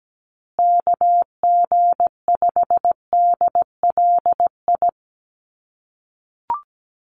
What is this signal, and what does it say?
Morse code, heard as a keyed tone, about 17 wpm, 700 Hz. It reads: KG5DLI
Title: USA Call Signs Repeated 3x 17wpm